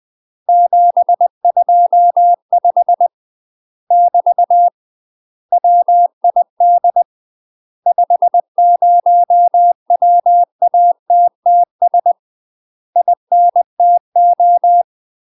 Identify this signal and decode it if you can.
Morse code, heard as a keyed tone, about 20 wpm, 700 Hz. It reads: 725 <BT> WID 50WATTS INTO
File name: UK MASTER QSOs TRAINING / 20WPM / UK MASTER QSOs 0045